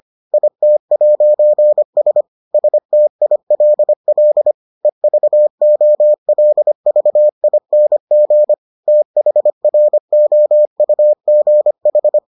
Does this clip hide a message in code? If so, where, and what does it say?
Yes: Morse code from 0.3 to 12.2 s, heard as a keyed tone, about 25 wpm, 600 Hz. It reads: IT'S STILL EVOLVING THROUGH